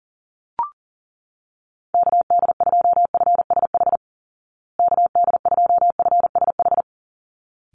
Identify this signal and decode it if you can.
Morse code, heard as a keyed tone, about 40 wpm, 700 Hz. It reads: KD2FSH KD2FSH
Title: Call Signs Repeated 3x - 40wpm